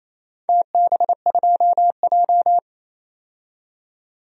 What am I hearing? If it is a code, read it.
Morse code, heard as a keyed tone, about 28 wpm, 700 Hz. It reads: TB2J
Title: DXCC-100 Call Signs - 4 Characters - 28wpm